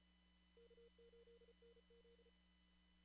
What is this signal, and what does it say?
Morse code, heard as a keyed tone, about 34 wpm, 470 Hz. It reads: K8NZ
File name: CW / kt5j wpx cw-op-k5pi